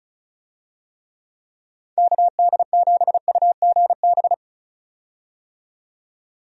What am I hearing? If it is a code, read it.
Morse code, heard as a keyed tone, about 34 wpm, 700 Hz. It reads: KD7UGB
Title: USA Call Signs - 6 Characters - 35wpm